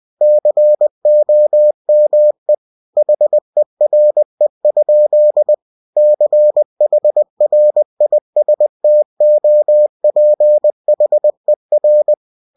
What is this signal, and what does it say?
Morse code, heard as a keyed tone, about 20 wpm, 600 Hz. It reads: COME HERE? CHRISTOPHER